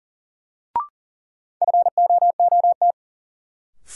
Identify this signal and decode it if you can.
Morse code, heard as a keyed tone, about 40 wpm, 700 Hz. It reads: FOOT
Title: Sentences from Top 500 Words - New Word Drill - 40wpm